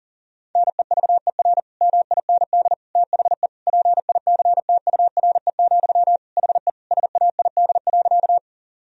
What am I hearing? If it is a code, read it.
Morse code, heard as a keyed tone, about 40 wpm, 700 Hz. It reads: NEVER MIND THE PICTURE, HE SAID.
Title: Sentences from Top 1000 Words - Review Part 1 - 40wpm